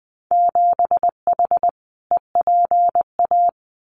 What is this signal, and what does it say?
Morse code, heard as a keyed tone, about 20 wpm, 700 Hz. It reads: 7H EPA
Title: Winter Field Day Exchanges Encore 20wpm